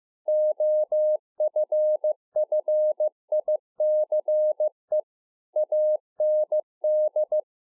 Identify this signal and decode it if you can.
Morse code, heard as a keyed tone, about 15 wpm, 610 Hz. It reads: OFFICE AND